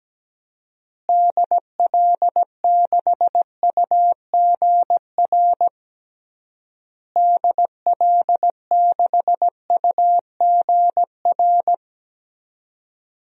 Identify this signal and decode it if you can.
Morse code, heard as a keyed tone, about 17 wpm, 700 Hz. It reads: DL6UGR DL6UGR
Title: DXCC-100 Call Signs - 6 Characters - Repeated 3x - 17wpm